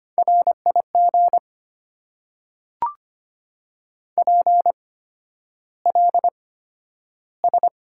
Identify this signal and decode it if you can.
Morse code, heard as a keyed tone, about 25 wpm, spaced as for 11 wpm, 700 Hz. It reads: RIG PLS
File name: Top 90 QSO Elements - 8x Character Spacing - 25wpm